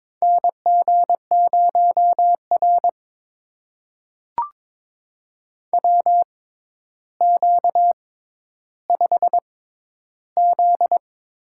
Morse code, heard as a keyed tone, about 22 wpm, spaced as for 11 wpm, 700 Hz. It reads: NG0R WQ5Z
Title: USA Call Signs - 4 Characters - 6x Character Spacing - 22wpm